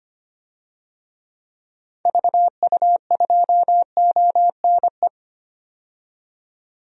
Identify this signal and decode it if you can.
Morse code, heard as a keyed tone, about 25 wpm, 700 Hz. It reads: VU2ONE
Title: DXCC-100 Call Signs - 6 Characters - 25wpm